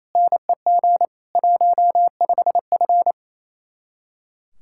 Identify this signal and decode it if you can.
Morse code, heard as a keyed tone, about 34 wpm, 700 Hz. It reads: NEG 15F
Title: Speed-Racing Common QSO Phrases - 28wpm